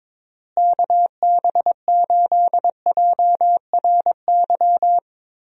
Morse code, heard as a keyed tone, about 22 wpm, 700 Hz. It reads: KB8JRY
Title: USA Call Signs - 6 Characters - Repeated 2x - 22wpm